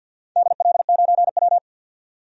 Morse code, heard as a keyed tone, about 51 wpm, 700 Hz. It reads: DL9W
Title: DXCC-100 Call Signs - 4 Characters - 50wpm